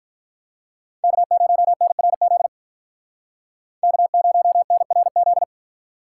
Morse code, heard as a keyed tone, about 45 wpm, 700 Hz. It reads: K0NRZ K0NRZ